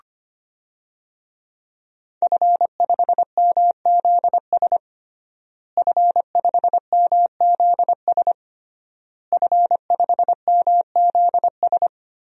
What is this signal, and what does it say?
Morse code, heard as a keyed tone, about 25 wpm, 700 Hz. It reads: F5MZS F5MZS F5MZS
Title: DXCC-100 Call Signs - 5 Characters - Repeated 3x - 25wpm